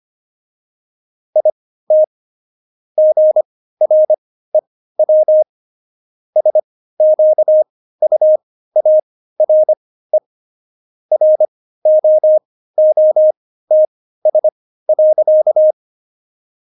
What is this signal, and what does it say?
Morse code, heard as a keyed tone, about 25 wpm, spaced as for 15 wpm, 620 Hz. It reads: IT GREW SQUARE ROOTS.
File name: Jokes-25-char-15-eff